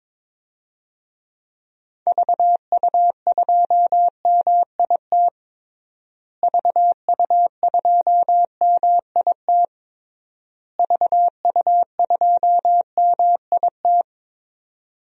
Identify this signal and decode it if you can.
Morse code, heard as a keyed tone, about 22 wpm, 700 Hz. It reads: VU2MIT VU2MIT VU2MIT
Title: DXCC-100 Call Signs - 6 Characters - Repeated 3x - 22wpm